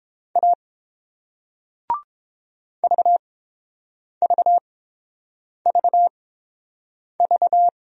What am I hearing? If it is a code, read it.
Morse code, heard as a keyed tone, about 32 wpm, spaced as for 7 wpm, 690 Hz. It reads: A VVVV